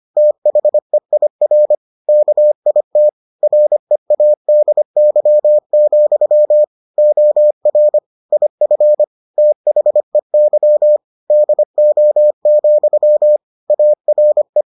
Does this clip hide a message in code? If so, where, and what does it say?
Yes: Morse code from 0.2 to 14.6 s, heard as a keyed tone, about 25 wpm, 600 Hz. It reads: THEIR KIT READY, OR IF THEY DO, ARE